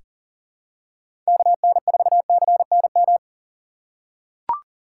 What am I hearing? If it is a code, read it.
Morse code, heard as a keyed tone, about 40 wpm, 700 Hz. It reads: KN4CNM
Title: USA Call Signs - 2x Character Spacing - 40wpm